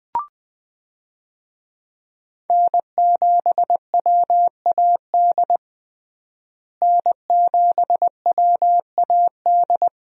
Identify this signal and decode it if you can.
Morse code, heard as a keyed tone, about 21 wpm, 700 Hz. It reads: N7WAD N7WAD
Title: USA Call Signs - 5 Characters - Repeated 2x - 20wpm